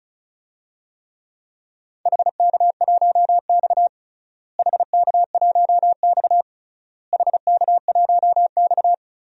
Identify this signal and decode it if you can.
Morse code, heard as a keyed tone, about 36 wpm, 700 Hz. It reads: HK1X HK1X HK1X
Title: DXCC-100 Call Signs - 4 Characters - Repeated 3x - 35wpm